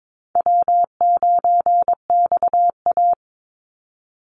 Morse code, heard as a keyed tone, about 22 wpm, 700 Hz. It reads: W9XA
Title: Call Signs - 22wpm